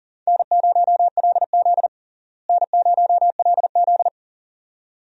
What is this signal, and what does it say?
Morse code, heard as a keyed tone, about 40 wpm, 700 Hz. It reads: N0LZ N0LZ
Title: USA Call Signs - 4 Characters - Repeated 2x - 40wpm